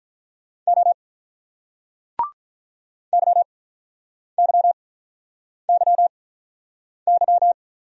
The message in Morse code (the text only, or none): O YYYY